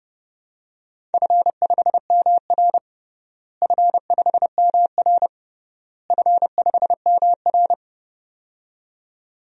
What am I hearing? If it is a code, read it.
Morse code, heard as a keyed tone, about 30 wpm, 700 Hz. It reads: F5MR F5MR F5MR
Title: DXCC-100 Call Signs - 4 Characters - Repeated 3x - 30wpm